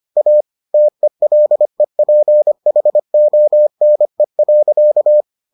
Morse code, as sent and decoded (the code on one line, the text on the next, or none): .- / - . .-.. .--. .... --- -. .-.-.-
A TELEPHONE.